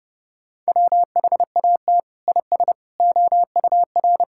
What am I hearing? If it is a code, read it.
Morse code, heard as a keyed tone, about 30 wpm, 700 Hz. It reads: WHAT IS OUR
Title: Sentences from Top 1000 Words - Review Part 6 - 30wpm